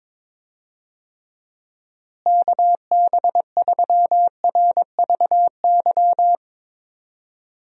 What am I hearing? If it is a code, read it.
Morse code, heard as a keyed tone, about 22 wpm, 700 Hz. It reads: KB3RVY
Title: USA Call Signs - 6 Characters - Repeated 3x - 22wpm